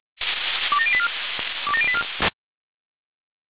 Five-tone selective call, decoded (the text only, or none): 27084 27084